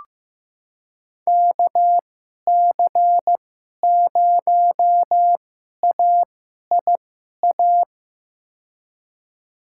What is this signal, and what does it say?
Morse code, heard as a keyed tone, about 15 wpm, spaced as for 11 wpm, 700 Hz. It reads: KC0AIA
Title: USA Call Signs - 2x Character Spacing - 15wpm